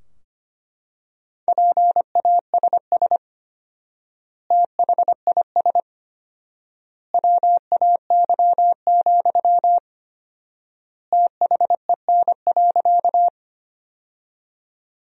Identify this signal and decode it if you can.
Morse code, heard as a keyed tone, about 25 wpm, 700 Hz. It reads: PASS THIS WAY, THEN.
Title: 4x Word-Spacing - Sentences from Top 400 Words - 25wpm